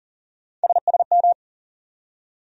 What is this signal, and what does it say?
Morse code, heard as a keyed tone, about 40 wpm, 700 Hz. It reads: SSM
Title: USA Call Sign Postfixes - 3 Letters 40wpm